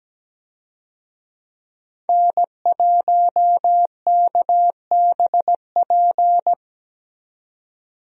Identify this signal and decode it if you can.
Morse code, heard as a keyed tone, about 17 wpm, 700 Hz. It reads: N1KBP
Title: USA Call Signs - 5 Characters - 17wpm